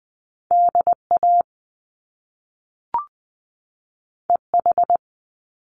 Morse code, heard as a keyed tone, about 20 wpm, 700 Hz. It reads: DA EH
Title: Top Two Letter Combinations in Words 20wpm